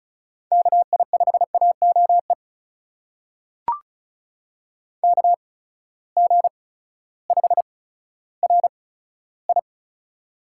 Morse code, heard as a keyed tone, about 36 wpm, spaced as for 16 wpm, 700 Hz. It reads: KI5AOE KG5RI
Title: USA Call Signs - 6 Characters - 8x Character Spacing - 35wpm